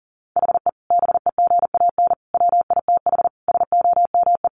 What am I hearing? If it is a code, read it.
Morse code, heard as a keyed tone, about 40 wpm, 700 Hz. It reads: HE BEGAN WITH SOME
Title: Sentences from Top 1500 Words 40wpm